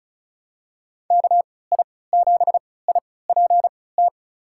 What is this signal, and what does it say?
Morse code, heard as a keyed tone, about 36 wpm, spaced as for 20 wpm, 700 Hz. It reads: KI7IPT
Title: USA Call Signs - 3x Character Spacing - 35wpm